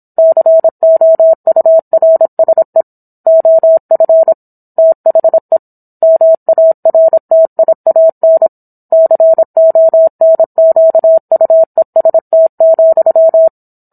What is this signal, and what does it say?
Morse code, heard as a keyed tone, about 27 wpm, 650 Hz. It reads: COURSE OF THE MARTIAN CONQUEST,